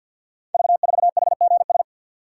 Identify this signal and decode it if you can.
Morse code, heard as a keyed tone, about 50 wpm, 700 Hz. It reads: V4HGS
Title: Code Groups - 5 Letter:Numbers - 50wpm